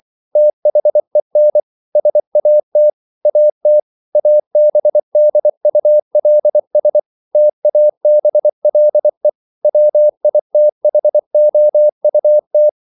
Morse code, heard as a keyed tone, about 24 wpm, 600 Hz. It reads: THEN SAT AT ABDULS TABLE WITHOUT